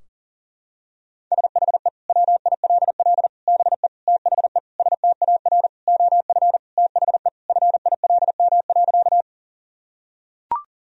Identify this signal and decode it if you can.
Morse code, heard as a keyed tone, about 40 wpm, 700 Hz. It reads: SHE WILL BE THE STAR OF THE FILM.
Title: Sentences from Top 1000 Words - Review Part 3 - 40wpm